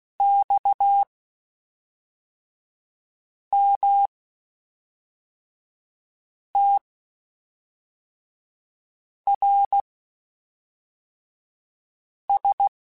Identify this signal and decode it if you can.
Morse code, heard as a keyed tone, about 16 wpm, spaced as for 3 wpm, 800 Hz. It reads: XMTRS
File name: (spaced) / QSOs Spaced-mp3s / WA7QOX 1 @5 sp